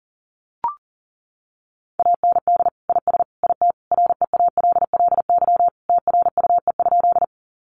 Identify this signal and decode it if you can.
Morse code, heard as a keyed tone, about 40 wpm, 700 Hz. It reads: AND IS IT REALLY TRUE?
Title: Sentences from Top 600 Words 40wpm